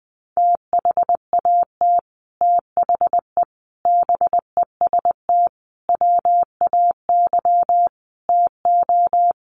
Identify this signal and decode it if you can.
Morse code, heard as a keyed tone, about 20 wpm, 700 Hz. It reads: THAT THE BEST WAY TO
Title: Sets of 5 Words Encore2 20wpm